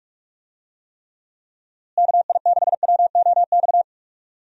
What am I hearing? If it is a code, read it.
Morse code, heard as a keyed tone, about 46 wpm, 700 Hz. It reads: KI6WOX
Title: USA Call Signs - 6 Characters - Repeated 2x - 45wpm